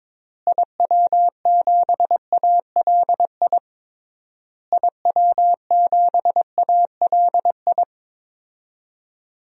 Morse code, heard as a keyed tone, about 22 wpm, 700 Hz. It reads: IW7ALI IW7ALI
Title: DXCC-100 Call Signs - 6 Characters - Repeated 3x - 22wpm